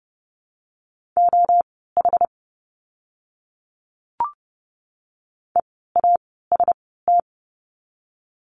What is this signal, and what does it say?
Morse code, heard as a keyed tone, about 30 wpm, spaced as for 17 wpm, 700 Hz. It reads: OH EAST
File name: Lesson 16f - Letter c - Words 30wpm